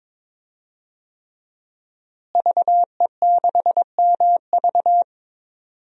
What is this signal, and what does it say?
Morse code, heard as a keyed tone, about 22 wpm, 700 Hz. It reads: VE6MV
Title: DXCC-100 Call Signs - 5 Characters - Repeated 3x - 22wpm